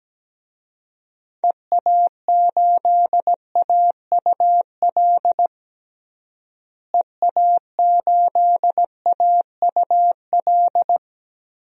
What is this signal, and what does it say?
Morse code, heard as a keyed tone, about 17 wpm, 700 Hz. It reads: EA8AUL EA8AUL
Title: DXCC-100 Call Signs - 6 Characters - Repeated 3x - 17wpm